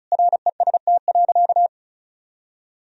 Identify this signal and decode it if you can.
Morse code, heard as a keyed tone, about 34 wpm, 700 Hz. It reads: REST.
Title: Sentences from Top 1000 Words - Review Part 1 - 35wpm